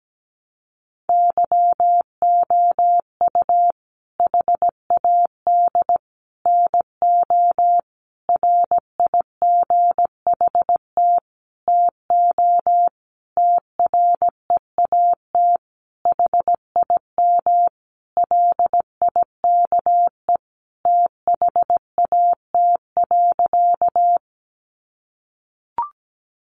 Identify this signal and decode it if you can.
Morse code, heard as a keyed tone, about 17 wpm, 700 Hz. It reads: YOU HAD NO RIGHT TO TREAT HIM LIKE THAT.
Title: Sentences from Top 1300 Words 17wpm